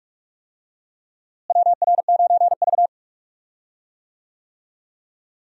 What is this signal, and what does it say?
Morse code, heard as a keyed tone, about 46 wpm, 700 Hz. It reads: WR9V
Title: USA Call Signs - 4 Characters - Repeated 3x - 45wpm